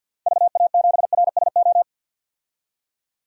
Morse code, heard as a keyed tone, about 47 wpm, 700 Hz. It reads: VA7RSO
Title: DXCC-100 Call Signs - 6 Characters - 50wpm